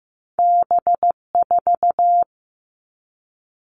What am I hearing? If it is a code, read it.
Morse code, heard as a keyed tone, about 15 wpm, 700 Hz. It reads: B4
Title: All Cut Numbers - 2-Digit 15wpm